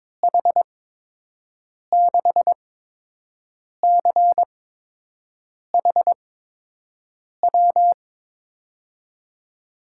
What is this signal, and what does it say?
Morse code, heard as a keyed tone, about 22 wpm, spaced as for 6 wpm, 700 Hz. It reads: H6CHW
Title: USA Call Signs - 6 Characters - 8x Character Spacing - 22wpm